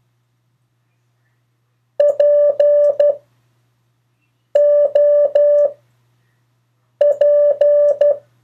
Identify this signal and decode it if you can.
Morse code, heard as a keyed tone, about 12 wpm, 580 Hz. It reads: P O P
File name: 3of4-words-2